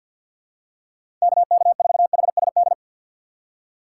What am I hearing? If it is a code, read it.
Morse code, heard as a keyed tone, about 53 wpm, 700 Hz. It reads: KK4HSD
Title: USA Call Signs - 4x Character Spacing - 50wpm